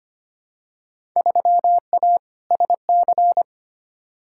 Morse code, heard as a keyed tone, about 25 wpm, 700 Hz. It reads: 3A SC
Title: Field Day Exchanges Encore - 25wpm